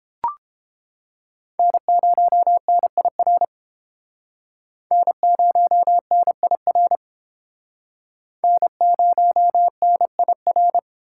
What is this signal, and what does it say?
Morse code, heard as a keyed tone, about 30 wpm, 700 Hz. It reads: N0NIR N0NIR N0NIR